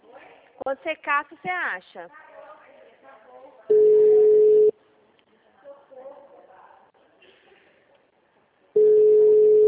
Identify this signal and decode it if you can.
Ringback tone (Europe, 425 Hz), with speech.
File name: out- -2024- - - .58593